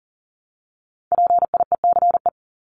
Morse code, heard as a keyed tone, about 40 wpm, 700 Hz. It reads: PIECE
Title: Top 500 Words 40wpm